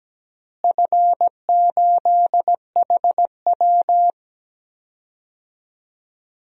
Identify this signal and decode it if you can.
Morse code, heard as a keyed tone, about 17 wpm, 700 Hz. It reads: F8HW
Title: DXCC-100 Call Signs - 4 Characters - Repeated 3x - 17wpm